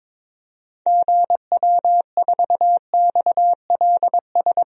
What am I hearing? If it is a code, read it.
Morse code, heard as a keyed tone, about 22 wpm, 700 Hz. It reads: GW4XLS
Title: DXCC-100 Call Signs - 6 Characters - Repeated 3x - 22wpm